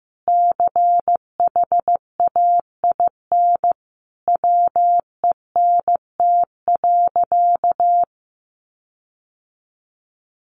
Morse code, heard as a keyed tone, about 15 wpm, 700 Hz. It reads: CHAIN WENT.
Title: Sentences from Top 1500 Words 15wpm